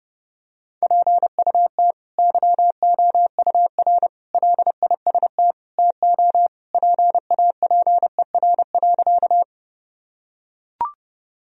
Morse code, heard as a keyed tone, about 30 wpm, 700 Hz. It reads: PUT YOUR LIST TO PAPER.
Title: Sentences from Top 1000 Words - Review Part 2 - 30wpm